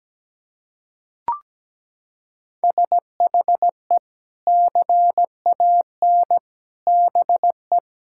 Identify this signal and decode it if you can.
Morse code, heard as a keyed tone, about 17 wpm, 700 Hz. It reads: SHE CAN BE
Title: Sentences from Top 1000 Words - Review Part 6 - 17wpm